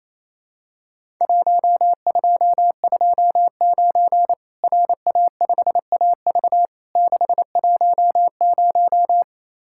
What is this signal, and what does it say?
Morse code, heard as a keyed tone, about 28 wpm, 700 Hz. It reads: 1229 RA5AV 610
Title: Worked All Europe DX Contest - QTC Exchange - 28wpm